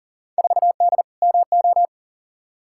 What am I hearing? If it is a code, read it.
Morse code, heard as a keyed tone, about 40 wpm, 700 Hz. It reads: 4D MO
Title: Field Day Exchanges Encore - 40wpm